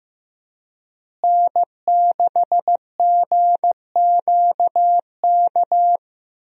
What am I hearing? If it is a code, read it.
Morse code, heard as a keyed tone, about 15 wpm, 700 Hz. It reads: N6GQK
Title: USA Call Signs - 5 Characters - Repeated 2x - 15wpm